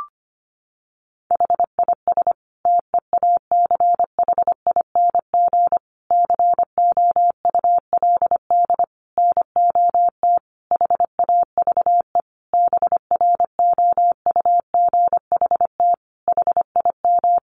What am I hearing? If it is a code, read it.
Morse code, heard as a keyed tone, about 25 wpm, 700 Hz. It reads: HIS TEACHING COULD NOT HAVE BROUGHT HIM
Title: Sentences from Top 1100 Words 25wpm